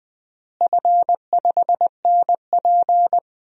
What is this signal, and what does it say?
Morse code, heard as a keyed tone, about 20 wpm, 700 Hz. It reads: F5NP
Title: DXCC-100 Call Signs - 4 Characters - Repeated 3x - 20wpm